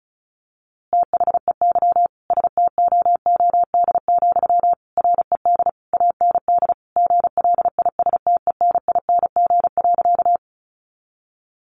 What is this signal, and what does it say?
Morse code, heard as a keyed tone, about 34 wpm, 700 Hz. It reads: THEY STOOD, RED AND GLISTENING.
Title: A Sound Of Thunder 35wpm